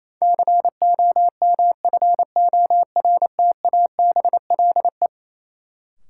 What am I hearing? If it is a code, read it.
Morse code, heard as a keyed tone, about 28 wpm, 700 Hz. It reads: COMFORTABLE
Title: Sentences from Top 1000 Words - Review Part 7 - New Word Drill - 28wpm